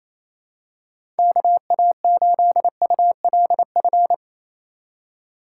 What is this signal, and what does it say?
Morse code, heard as a keyed tone, about 28 wpm, 700 Hz. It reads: KA8ULF
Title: USA Call Signs - 6 Characters - 4x Character Spacing - 28wpm